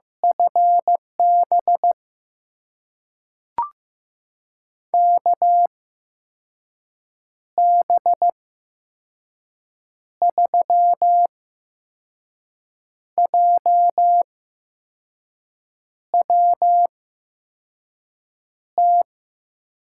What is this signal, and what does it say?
Morse code, heard as a keyed tone, about 15 wpm, spaced as for 5 wpm, 700 Hz. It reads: FB KB3JWT
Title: USA Call Signs - 6 Characters - 8x Character Spacing - 15wpm